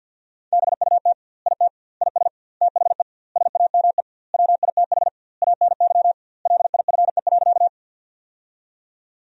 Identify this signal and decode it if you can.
Morse code, heard as a keyed tone, about 52 wpm, 700 Hz. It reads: BUT IT IS THE SAME WITH ANY LIFE.
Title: 2x Word-Spacing - Sentences from Top 300 Words - 50wpm